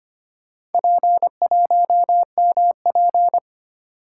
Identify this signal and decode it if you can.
Morse code, heard as a keyed tone, about 25 wpm, 700 Hz. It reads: P1MP
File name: practice / 25wpm / Four Letter-Number Combinations - 25wpm